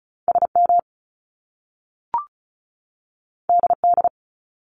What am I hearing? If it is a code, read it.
Morse code, heard as a keyed tone, about 36 wpm, 700 Hz. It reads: SM DD
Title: Top Two Letter Combinations in Words 35wpm